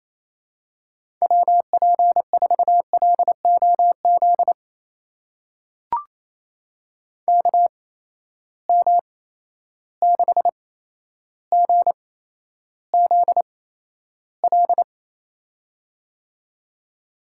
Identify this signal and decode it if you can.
Morse code, heard as a keyed tone, about 28 wpm, spaced as for 12 wpm, 700 Hz. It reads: WP4LOZ KM6GZL
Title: USA Call Signs - 6 Characters - 8x Character Spacing - 28wpm